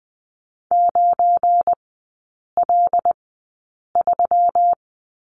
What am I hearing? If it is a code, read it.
Morse code, heard as a keyed tone, about 20 wpm, spaced as for 8 wpm, 700 Hz. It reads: 9L3
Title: 184 Lesson 30e - Letter b - Three Characters 20wpm